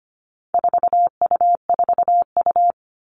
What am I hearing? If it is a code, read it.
Morse code, heard as a keyed tone, about 25 wpm, 700 Hz. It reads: 4U4U
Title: All Cut Numbers - 4-Digit 25wpm